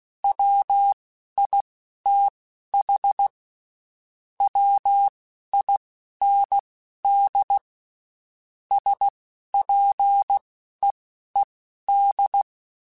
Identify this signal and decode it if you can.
Morse code, heard as a keyed tone, about 16 wpm, spaced as for 12 wpm, 800 Hz. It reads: WITH WIND SPEED
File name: Spaced Characters / Spaced Characters-mp3s / Weather 5 22 sp @15